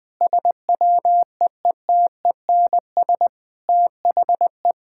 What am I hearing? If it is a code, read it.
Morse code, heard as a keyed tone, about 21 wpm, 700 Hz. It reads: SWEETENS THE
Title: Sentences with 2-Word Phrases - 20wpm